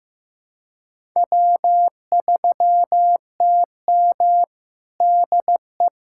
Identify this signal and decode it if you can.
Morse code, heard as a keyed tone, about 16 wpm, 700 Hz. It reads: W3TM DE
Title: USA Callsign DE Callsign - 4 Characters - 15wpm